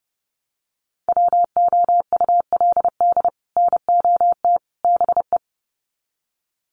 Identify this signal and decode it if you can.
Morse code, heard as a keyed tone, about 30 wpm, 700 Hz. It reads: WOULD NOT BE
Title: Sets of 3 Words 30wpm